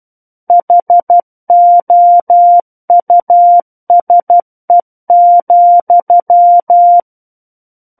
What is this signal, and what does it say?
Morse code, heard as a keyed tone, about 12 wpm, 700 Hz. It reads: HOUSE,
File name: Book One / 03-Chapter 3 - 12 WPM